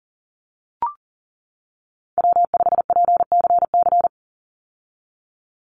Morse code, heard as a keyed tone, about 40 wpm, 700 Hz. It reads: W5PCC
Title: USA Call Signs 40wpm